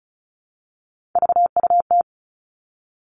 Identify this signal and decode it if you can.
Morse code, heard as a keyed tone, about 36 wpm, 700 Hz. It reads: VUT
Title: Three Challenging Letter-Number Combinations 35wpm